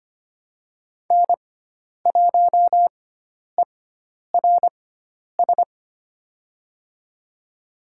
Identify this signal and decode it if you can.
Morse code, heard as a keyed tone, about 25 wpm, spaced as for 10 wpm, 700 Hz. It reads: N1ERS
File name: Lesson 16g - Letter c - Call Signs 25wpm